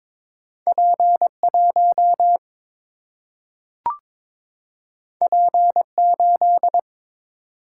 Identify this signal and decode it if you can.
Morse code, heard as a keyed tone, about 22 wpm, 700 Hz. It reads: P 1 P 8
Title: Single-Letter Single-Number 22wpm